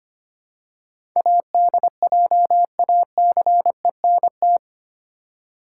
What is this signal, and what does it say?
Morse code, heard as a keyed tone, about 25 wpm, 700 Hz. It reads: ADJACENT
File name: 8 Letter Words - First 3 Letters Given - 25wpm